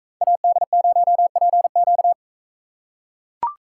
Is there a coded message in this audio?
Yes: Morse code from 0.2 to 2.1 s, heard as a keyed tone, about 43 wpm, 700 Hz. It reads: AD0PQ